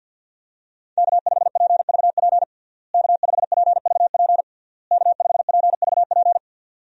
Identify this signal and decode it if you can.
Morse code, heard as a keyed tone, about 51 wpm, 700 Hz. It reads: K5PVP K5PVP K5PVP
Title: USA Call Signs - 5 Characters - Repeated 3x - 50wpm